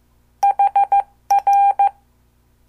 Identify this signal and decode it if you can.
Morse code, heard as a keyed tone, about 14 wpm, 800 Hz. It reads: HR
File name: morse / qso